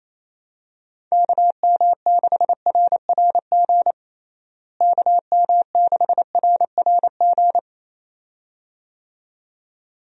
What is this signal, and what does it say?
Morse code, heard as a keyed tone, about 28 wpm, 700 Hz. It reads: KM6RRG KM6RRG
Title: USA Call Signs - 6 Characters - Repeated 2x - 28wpm